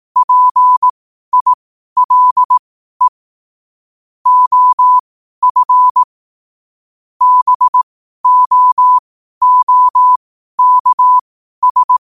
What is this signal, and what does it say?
Morse code, heard as a keyed tone, about 18 wpm, spaced as for 12 wpm, 1000 Hz. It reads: PILE OF BOOKS
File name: qotdfiles / 12wpm